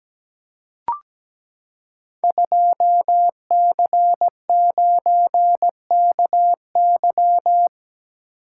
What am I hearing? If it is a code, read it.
Morse code, heard as a keyed tone, about 17 wpm, 700 Hz. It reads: 2C9KY
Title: Code Groups - 5 Letter:Numbers - 17wpm